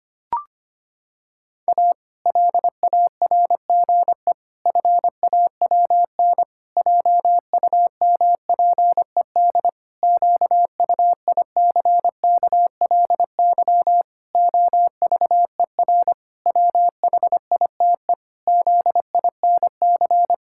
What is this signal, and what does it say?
Morse code, heard as a keyed tone, about 25 wpm, 700 Hz. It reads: A LARGE FAWN JUMPED QUICKLY OVER WHITE ZINC
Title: Pangram Challenge v2 - 25wpm